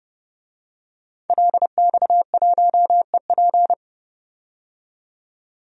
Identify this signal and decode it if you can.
Morse code, heard as a keyed tone, about 30 wpm, 700 Hz. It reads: LX1EP
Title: DXCC-100 Call Signs - 5 Characters - 30wpm